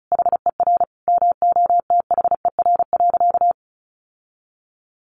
Morse code, heard as a keyed tone, about 34 wpm, 700 Hz. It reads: HER MOTHER.
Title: Sentences from Top 1500 Words 35wpm